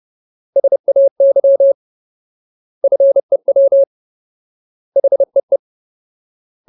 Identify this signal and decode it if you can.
Morse code, heard as a keyed tone, about 30 wpm, 550 Hz. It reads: SAY FEW HEE
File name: Ultimate Word List / 30WPM Ultimate